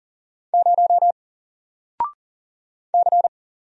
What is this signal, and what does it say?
Morse code, heard as a keyed tone, about 40 wpm, 700 Hz. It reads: 0 C
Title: Single Letter-Number 40wpm